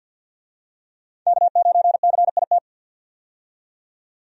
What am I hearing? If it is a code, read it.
Morse code, heard as a keyed tone, about 50 wpm, 700 Hz. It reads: K9CIT
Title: USA Call Signs - 5 Characters - 50wpm